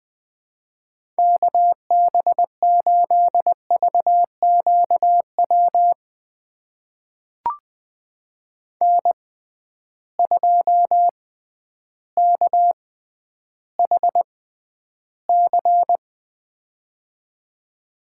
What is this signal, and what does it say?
Morse code, heard as a keyed tone, about 21 wpm, spaced as for 8 wpm, 700 Hz. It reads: KB8VQ WN2KHC
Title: USA Call Signs - 6x Character Spacing - 20wpm